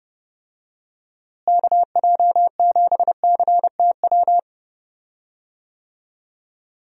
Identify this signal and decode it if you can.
Morse code, heard as a keyed tone, about 32 wpm, 700 Hz. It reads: KJ7CTW
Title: USA Call Signs - 6 Characters - Repeated 3x - 30wpm